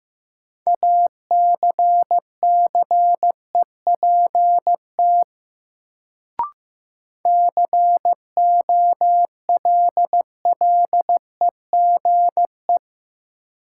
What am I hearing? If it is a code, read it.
Morse code, heard as a keyed tone, about 15 wpm, 700 Hz. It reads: ACCEPT COLLEGE
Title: Sentences from Top 1300 Words - New Word Drill - 15wpm